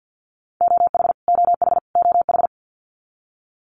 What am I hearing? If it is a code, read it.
Morse code, heard as a keyed tone, about 50 wpm, 700 Hz. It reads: OH OH OH
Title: Top 500 Words Repeated 50wpm